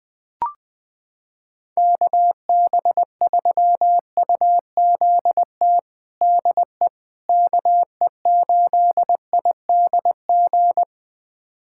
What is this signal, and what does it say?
Morse code, heard as a keyed tone, about 20 wpm, 700 Hz. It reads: KB3UZT DE KE8IDG
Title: USA Callsign DE Callsign 20wpm